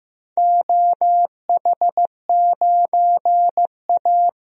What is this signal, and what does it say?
Morse code, heard as a keyed tone, about 15 wpm, 700 Hz. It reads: OH9A
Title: DXCC-100 Call Signs - 4 Characters - Repeated 2x - 15wpm